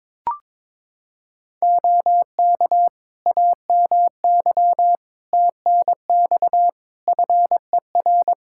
Morse code, heard as a keyed tone, about 22 wpm, 700 Hz. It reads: OK AMY TNX FER
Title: Practice QSO - Exchange 4 - Ending - 22wpm